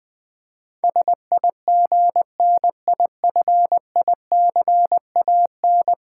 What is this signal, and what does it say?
Morse code, heard as a keyed tone, about 20 wpm, 700 Hz. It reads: SIGNIFICAN
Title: Build A Word - 5 Letters At A Time - 12-Letter Words - 20wpm